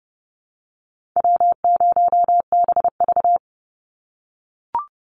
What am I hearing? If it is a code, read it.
Morse code, heard as a keyed tone, about 30 wpm, 700 Hz. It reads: W0BV
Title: Easy Call Signs - 30wpm